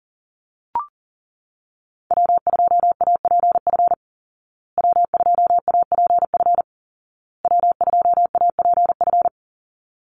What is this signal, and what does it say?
Morse code, heard as a keyed tone, about 40 wpm, 700 Hz. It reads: W2APF W2APF W2APF
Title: Call Signs Repeated 2x - 40wpm